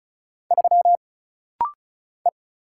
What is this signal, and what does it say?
Morse code, heard as a keyed tone, about 35 wpm, 700 Hz. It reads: 3 E
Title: Single Letter-Number - Rapid-Fire 35wpm